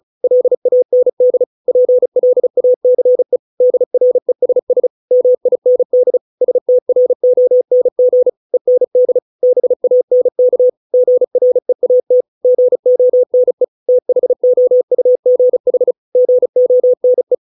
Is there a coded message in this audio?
Yes: Morse code from 0.2 to 17.3 s, heard as a keyed tone, about 34 wpm, 500 Hz. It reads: LAND PLACE DRESS MIND STRONG END BANK GREAT GONE THOUGH GONE